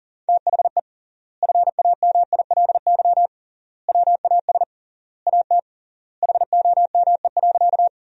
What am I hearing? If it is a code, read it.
Morse code, heard as a keyed tone, about 40 wpm, 700 Hz. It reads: THE FAMILY WAS AT HOME.
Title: 3x Word-Spacing - Sentences from Top 400 Words - 40wpm